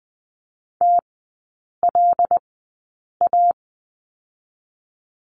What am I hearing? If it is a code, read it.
Morse code, heard as a keyed tone, about 20 wpm, spaced as for 8 wpm, 700 Hz. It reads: TLA
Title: 081 Lesson 14e - Number 2 - Three Characters 20wpm